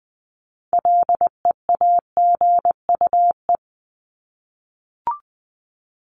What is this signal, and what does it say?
Morse code, heard as a keyed tone, about 20 wpm, 700 Hz. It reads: LEAGUE
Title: Top 1751-2000 Words 20wpm